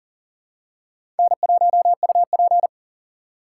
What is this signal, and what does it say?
Morse code, heard as a keyed tone, about 39 wpm, 700 Hz. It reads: N1UP
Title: USA Call Signs - 4 Characters - 6x Character Spacing - 40wpm